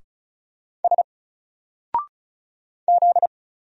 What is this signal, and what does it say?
Morse code, heard as a keyed tone, about 36 wpm, 700 Hz. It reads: S Z
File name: Single-Letter-Number-35wpm